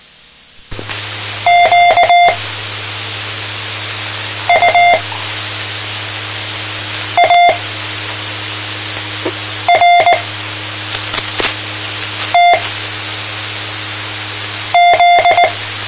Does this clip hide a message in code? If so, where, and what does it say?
Yes: Morse code from 1.5 to 15.4 s, heard as a keyed tone, about 19 wpm, spaced as for 4 wpm, 700 Hz. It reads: QUARTZ